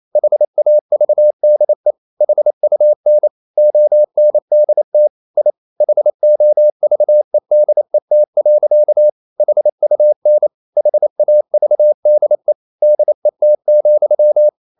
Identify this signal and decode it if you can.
Morse code, heard as a keyed tone, about 28 wpm, 600 Hz. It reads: HAVDE HUN ONDT I HOVEDET. HUN HAVDE DET,
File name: Ferie QRQ 28wpm